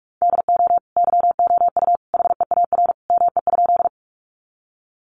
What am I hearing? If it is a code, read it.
Morse code, heard as a keyed tone, about 44 wpm, 700 Hz. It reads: DO YOU HEAR ME?
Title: Sentences from Top 300 Words 45wpm